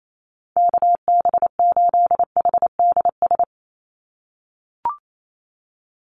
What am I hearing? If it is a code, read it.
Morse code, heard as a keyed tone, about 28 wpm, 700 Hz. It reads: KB8HDS
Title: USA Call Signs 28wpm